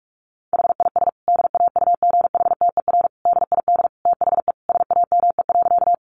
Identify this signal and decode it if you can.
Morse code, heard as a keyed tone, about 44 wpm, 700 Hz. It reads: HIS DAUGHTER DID THE SAME.
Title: Sentences from Top 1500 Words 45wpm